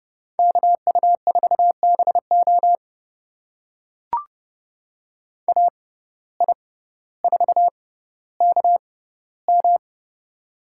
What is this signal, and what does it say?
Morse code, heard as a keyed tone, about 30 wpm, spaced as for 15 wpm, 700 Hz. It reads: KU4BO AI4KM
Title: USA Call Signs - 5 Characters - 6x Character Spacing - 30wpm